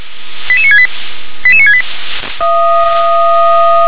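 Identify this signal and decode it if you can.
Five-tone selective call (ZVEI), heard as quick runs of five tones, tones of about 70 ms. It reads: 88968 88968
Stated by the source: Five-tone ZVEI selective call received by SDR.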